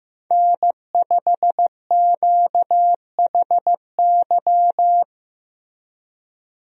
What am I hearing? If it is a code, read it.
Morse code, heard as a keyed tone, about 15 wpm, 700 Hz. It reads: N5QHY
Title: USA Call Signs - 5 Characters - Repeated 3x - 15wpm